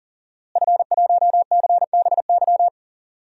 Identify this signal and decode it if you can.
Morse code, heard as a keyed tone, about 40 wpm, 700 Hz. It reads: F1CBY
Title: DXCC-100 Call Signs - 5 Characters - Repeated 3x - 40wpm